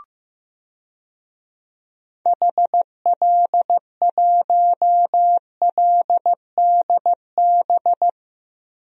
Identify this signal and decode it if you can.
Morse code, heard as a keyed tone, about 15 wpm, 700 Hz. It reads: HL1LDB
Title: DXCC-100 Call Signs - 6 Characters - 15wpm